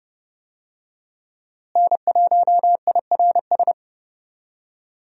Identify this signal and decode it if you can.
Morse code, heard as a keyed tone, about 30 wpm, 700 Hz. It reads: N1IRS
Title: USA Call Signs - 5 Characters - Repeated 3x - 30wpm